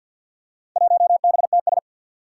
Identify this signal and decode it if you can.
Morse code, heard as a keyed tone, about 51 wpm, 700 Hz. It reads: 1BTS
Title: Four Challenging Letter-Number Combinations 50wpm